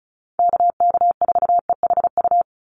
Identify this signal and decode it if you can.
Morse code, heard as a keyed tone, about 34 wpm, 700 Hz. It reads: KK4EHU
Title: USA Call Signs 35wpm